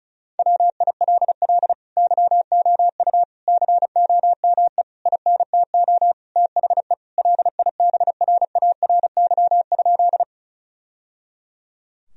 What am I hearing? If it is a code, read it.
Morse code, heard as a keyed tone, about 36 wpm, 700 Hz. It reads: WILL YOU COME INTO THE LIBRARY?
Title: Sentences from Top 1000 Words - Review Part 4 - 35wpm